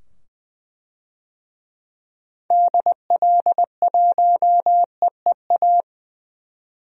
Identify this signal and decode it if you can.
Morse code, heard as a keyed tone, about 20 wpm, 700 Hz. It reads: DL1EEA
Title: DXCC-100 Call Signs - 6 Characters - 20wpm